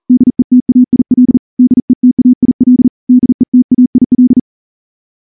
Morse code, heard as a keyed tone, about 38 wpm, 250 Hz. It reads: DETAIL DETAIL DETAIL
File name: morse-training-40WPM